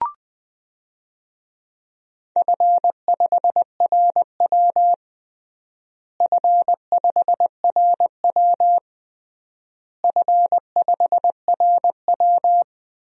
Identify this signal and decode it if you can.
Morse code, heard as a keyed tone, about 20 wpm, 700 Hz. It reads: F5RW F5RW F5RW
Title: DXCC-100 Call Signs - 4 Characters - Repeated 3x - 20wpm